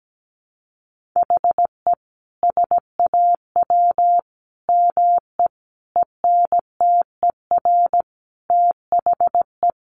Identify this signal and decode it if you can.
Morse code, heard as a keyed tone, about 17 wpm, 700 Hz. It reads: HE SAW ME ENTER THE
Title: Sentences from Top 1300 Words 17wpm